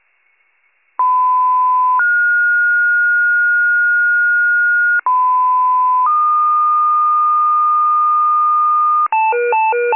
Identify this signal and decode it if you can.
Two-tone page, heard as a one-second tone, then a three-second tone, about 1010 then 1430 Hz, about 1010 then 1220 Hz, with noise. It is heard from 1.0 to 9.1 s.